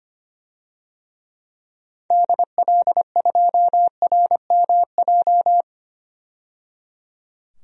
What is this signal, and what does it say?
Morse code, heard as a keyed tone, about 25 wpm, 700 Hz. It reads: DL2RMJ
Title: DXCC-100 Call Signs - 6 Characters - 25wpm